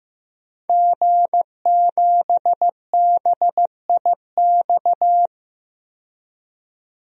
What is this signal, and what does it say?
Morse code, heard as a keyed tone, about 15 wpm, 700 Hz. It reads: G7BIX
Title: DXCC-100 Call Signs - 5 Characters - 15wpm